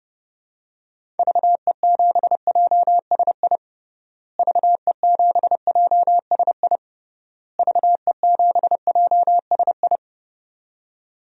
Morse code, heard as a keyed tone, about 30 wpm, 700 Hz. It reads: VE7JSI VE7JSI VE7JSI
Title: DXCC-100 Call Signs - 6 Characters - Repeated 3x - 30wpm